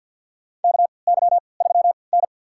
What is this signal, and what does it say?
Morse code, heard as a keyed tone, about 53 wpm, spaced as for 29 wpm, 700 Hz. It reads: KY3N
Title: USA Call Signs - 4 Characters - 3x Character Spacing - 50wpm